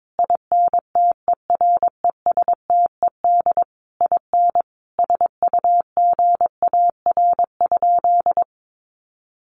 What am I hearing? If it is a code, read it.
Morse code, heard as a keyed tone, about 22 wpm, 700 Hz. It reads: INTERESTED IN SUGAR?
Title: Sentences from Top 1200 Words 22wpm